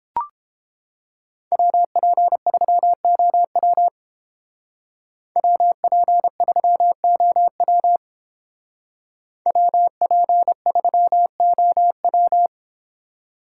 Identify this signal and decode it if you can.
Morse code, heard as a keyed tone, about 30 wpm, 700 Hz. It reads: WP3OW WP3OW WP3OW